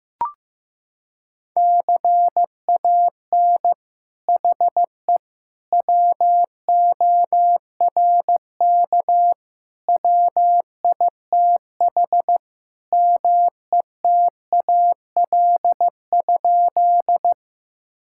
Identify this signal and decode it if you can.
Morse code, heard as a keyed tone, about 15 wpm, 700 Hz. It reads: CAN HE WORK WITH METAL?
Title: Sentences from Top 1000 Words - Review Part 6 - 15wpm